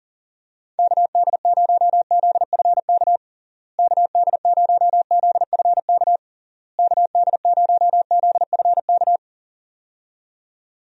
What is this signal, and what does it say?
Morse code, heard as a keyed tone, about 40 wpm, 700 Hz. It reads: KD0ZFK KD0ZFK KD0ZFK
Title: USA Call Signs - 6 Characters - Repeated 3x - 40wpm